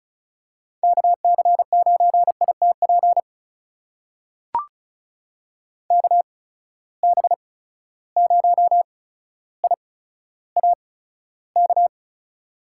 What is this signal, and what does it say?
Morse code, heard as a keyed tone, about 36 wpm, spaced as for 15 wpm, 700 Hz. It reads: KC9ITP KB0IAK
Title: USA Call Signs - 6 Characters - 8x Character Spacing - 35wpm